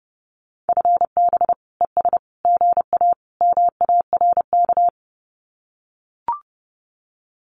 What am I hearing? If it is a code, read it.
Morse code, heard as a keyed tone, about 30 wpm, 700 Hz. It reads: FB ES GA MARK
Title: Common QSO Phrases Encore 30wpm